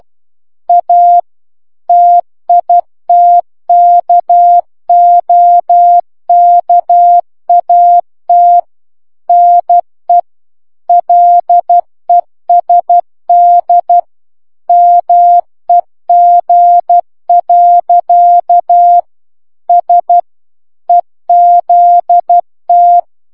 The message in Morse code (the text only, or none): A TITKOKAT NE LESD MEG. S EZT